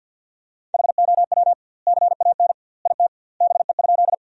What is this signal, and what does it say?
Morse code, heard as a keyed tone, about 50 wpm, 700 Hz. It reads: HOW CAN IT BE?
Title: 2x Word-Spacing - Sentences from Top 100 Words - 50wpm